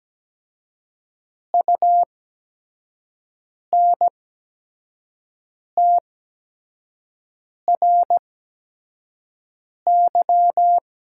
Morse code, heard as a keyed tone, about 17 wpm, spaced as for 5 wpm, 700 Hz. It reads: UNTRY
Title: 8x Character Spacing - Top 500 Words - 17wpm